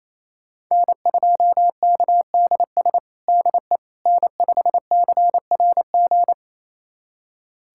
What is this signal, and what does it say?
Morse code, heard as a keyed tone, about 28 wpm, 700 Hz. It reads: N2KDS DE N5CRG
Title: USA Callsign DE Callsign - 5 Characters - 28wpm